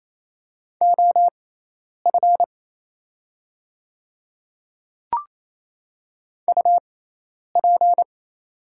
Morse code, heard as a keyed tone, about 28 wpm, spaced as for 10 wpm, 700 Hz. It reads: OF UP
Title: 6x Character Spacing - Top 100 Words - 28wpm